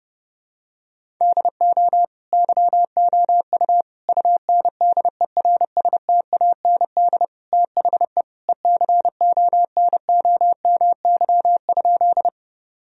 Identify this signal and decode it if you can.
Morse code, heard as a keyed tone, about 30 wpm, 700 Hz. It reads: DO YOU UNDERSTAND THE ECONOMY?
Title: Sentences from Top 1000 Words - Review Part 3 - 30wpm